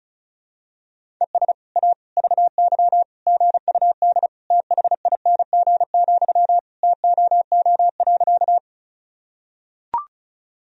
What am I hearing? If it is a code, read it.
Morse code, heard as a keyed tone, about 34 wpm, 700 Hz. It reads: ES A VY GUD THING, TOO.
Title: Sentences from Top 200 Words and Abbreviations 35wpm